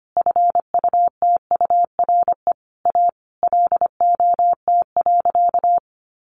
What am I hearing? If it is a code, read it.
Morse code, heard as a keyed tone, about 25 wpm, 700 Hz. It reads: FUTURE A LOT.
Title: Easy Phrases 25wpm